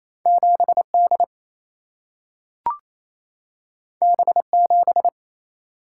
Morse code, heard as a keyed tone, about 28 wpm, 700 Hz. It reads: 7D B7
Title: Single-Letter Single-Number 28wpm